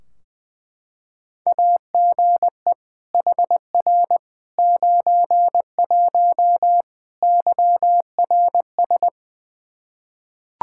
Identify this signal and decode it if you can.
Morse code, heard as a keyed tone, about 20 wpm, 700 Hz. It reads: AGE HR 91 YRS
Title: Practice QSO - Exchange 3 - More Info - 20wpm